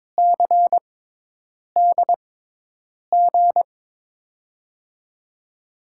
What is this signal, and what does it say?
Morse code, heard as a keyed tone, about 22 wpm, spaced as for 8 wpm, 700 Hz. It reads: CDG
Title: USA Call Signs - 5 Characters - 6x Character Spacing - 22wpm